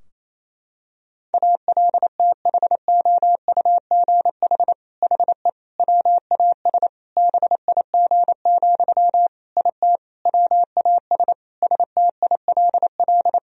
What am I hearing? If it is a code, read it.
Morse code, heard as a keyed tone, about 28 wpm, 700 Hz. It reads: ALTHOUGH HE WAS BIG, IT WAS STILL
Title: Sentences from Top 1000 Words - Review Part 3 - 28wpm